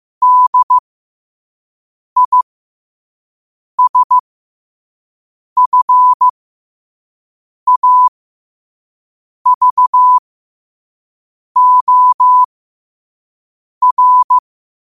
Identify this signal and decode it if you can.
Morse code, heard as a keyed tone, about 15 wpm, spaced as for 5 wpm, 1000 Hz. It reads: DISFAVOR